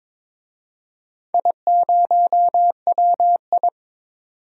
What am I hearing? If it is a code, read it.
Morse code, heard as a keyed tone, about 22 wpm, 700 Hz. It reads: I0WI
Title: DXCC-100 Call Signs - 4 Characters - 22wpm